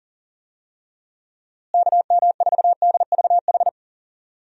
Morse code, heard as a keyed tone, about 40 wpm, 700 Hz. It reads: KM4DVH
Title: USA Call Signs - 6 Characters - Repeated 3x - 40wpm